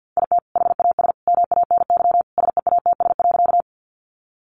Morse code, heard as a keyed tone, about 50 wpm, 700 Hz. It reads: IT HAS MANY SEATS.
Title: Additional Easy Phrases 50wpm